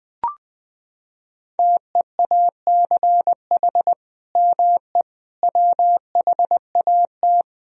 Morse code, heard as a keyed tone, about 21 wpm, 700 Hz. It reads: TEACH ME WHAT
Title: Sentences from Top 500 Words and Abbreviations 20wpm